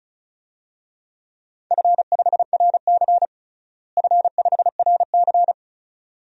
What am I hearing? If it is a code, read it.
Morse code, heard as a keyed tone, about 36 wpm, 700 Hz. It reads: F5RC F5RC
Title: DXCC-100 Call Signs - 4 Characters - Repeated 2x - 35wpm